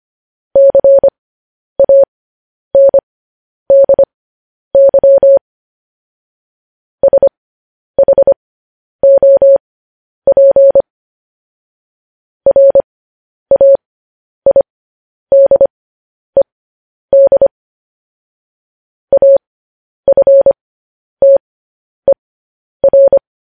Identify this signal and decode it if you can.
Morse code, heard as a keyed tone, about 25 wpm, spaced as for 10 wpm, 550 Hz. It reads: CANDY SHOP RAIDED AFTER
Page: news-headlines-in-morse-code-at-10-wpm